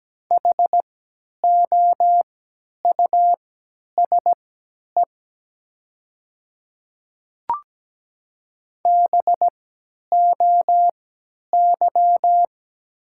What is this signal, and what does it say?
Morse code, heard as a keyed tone, about 17 wpm, spaced as for 10 wpm, 700 Hz. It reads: HOUSE BOY
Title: 3x Character Spacing - Top 200 Words - 17wpm